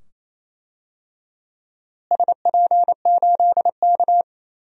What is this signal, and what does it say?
Morse code, heard as a keyed tone, about 28 wpm, 700 Hz. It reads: SP8K
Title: DXCC-100 Call Signs - 4 Characters - Repeated 2x - 28wpm